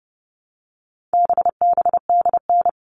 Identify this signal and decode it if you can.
Morse code, heard as a keyed tone, about 29 wpm, 700 Hz. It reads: BBDN
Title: All Cut Numbers - 4-Digit 30wpm